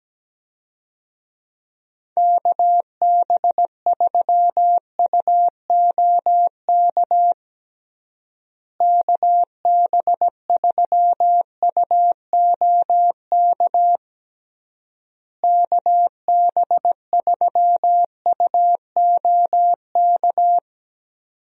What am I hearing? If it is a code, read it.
Morse code, heard as a keyed tone, about 17 wpm, 700 Hz. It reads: KB3UOK KB3UOK KB3UOK
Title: USA Call Signs - 6 Characters - Repeated 3x - 17wpm